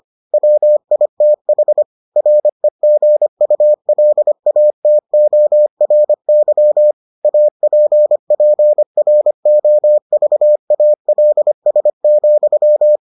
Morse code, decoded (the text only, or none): WITH REGULATORY APPROVALS,